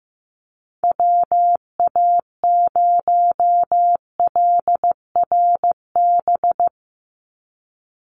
Morse code, heard as a keyed tone, about 15 wpm, 700 Hz. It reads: WA0LRB
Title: USA Call Signs 15wpm